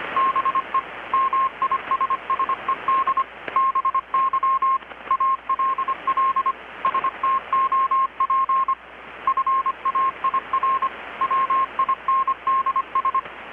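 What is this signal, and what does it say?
Morse code, heard as a keyed tone, about 25 wpm, 1070 Hz, with radio noise. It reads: BE MISSED BY ALL STOP FAIR WINDS
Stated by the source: Morse from KPH's Night of Nights XX maritime radiotelegraphy event, received in Chiapas, Mexico about 3,800 km away on a Kenwood R-600 with a grounded 10 m long-wire antenna.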